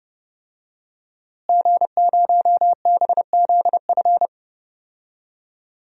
Morse code, heard as a keyed tone, about 30 wpm, 700 Hz. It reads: G0BZF
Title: DXCC-100 Call Signs - 5 Characters - Repeated 2x - 30wpm